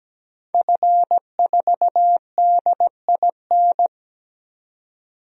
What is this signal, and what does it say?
Morse code, heard as a keyed tone, about 17 wpm, 700 Hz. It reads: F4DIN
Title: DXCC-100 Call Signs - 5 Characters - 17wpm